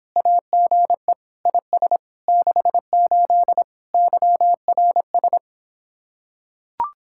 Morse code, heard as a keyed tone, about 26 wpm, 700 Hz. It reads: AGE IS 68 YRS